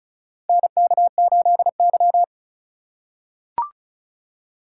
Morse code, heard as a keyed tone, about 36 wpm, 700 Hz. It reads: NK8Y
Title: USA Call Signs - 4 Characters - 4x Character Spacing - 35wpm